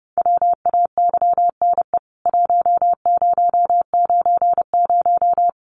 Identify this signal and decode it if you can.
Morse code, heard as a keyed tone, about 30 wpm, 700 Hz. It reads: WAYNE 1090
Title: CWT Exchange 30wpm